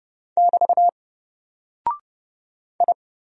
Morse code, heard as a keyed tone, about 31 wpm, 700 Hz. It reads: <BT> I
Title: Single Character (Letter:Number:Punctuation:Prosign) 30wpm